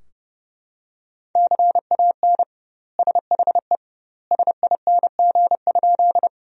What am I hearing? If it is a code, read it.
Morse code, heard as a keyed tone, about 30 wpm, 700 Hz. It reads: CAN SHE SING?
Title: 2x Word-Spacing - Sentences from Top 400 Words - 30wpm